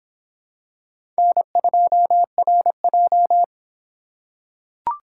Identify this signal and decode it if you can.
Morse code, heard as a keyed tone, about 26 wpm, 700 Hz. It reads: N2RJ